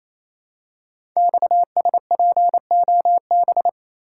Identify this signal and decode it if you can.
Morse code, heard as a keyed tone, about 28 wpm, 700 Hz. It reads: XSPOB
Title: Code Groups - 5 Letters - 28wpm